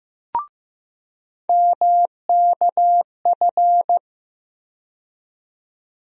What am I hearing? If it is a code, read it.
Morse code, heard as a keyed tone, about 15 wpm, 700 Hz. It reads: MKF
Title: USA Call Sign Postfixes - 3 Letters 15wpm